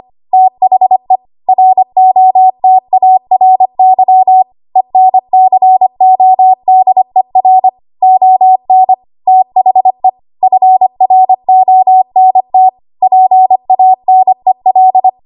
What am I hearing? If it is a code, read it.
Morse code, heard as a keyed tone, about 25 wpm, 750 Hz. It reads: THE ROTARY ENCODER ON THE FRONT PANEL
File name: Archive / 25 WPM / 25WPM